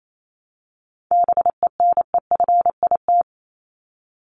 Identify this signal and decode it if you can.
Morse code, heard as a keyed tone, about 28 wpm, 700 Hz. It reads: BENEFIT
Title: Top 751-1000 Words 28wpm